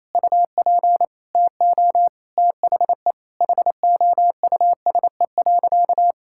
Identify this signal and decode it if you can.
Morse code, heard as a keyed tone, about 28 wpm, 700 Hz. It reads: UP TO THE HOUSE.
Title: Sentences from Top 1000 Words - Review Part 6 - 28wpm